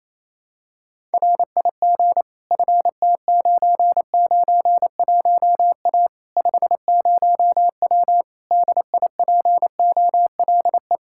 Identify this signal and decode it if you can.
Morse code, heard as a keyed tone, about 28 wpm, 700 Hz. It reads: RIG FT991A 50W DIPOLE
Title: Practice QSO - Exchange 2 - Info - 28wpm